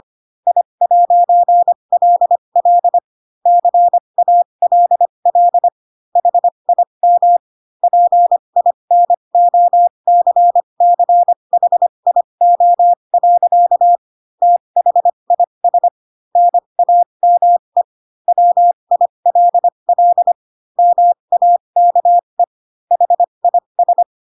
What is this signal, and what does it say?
Morse code, heard as a keyed tone, about 25 wpm, spaced as for 22 wpm, 700 Hz. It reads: I'LL CALL HIM PINOCCHIO. THIS NAME WILL MAKE HIS
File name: Pinocchio / Pinocchio ch3 18 25 ik1hge